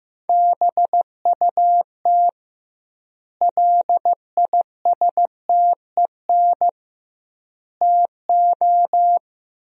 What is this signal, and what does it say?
Morse code, heard as a keyed tone, about 15 wpm, 700 Hz. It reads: BUT LISTEN TO
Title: 2x Word-Spacing - Sentences from Top 400 Words - 15wpm